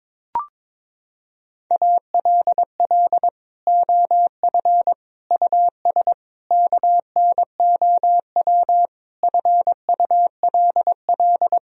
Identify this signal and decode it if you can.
Morse code, heard as a keyed tone, about 22 wpm, 700 Hz. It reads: ALL OF US KNOW FULL
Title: Sentences from Top 1000 Words - Review Part 3 - 22wpm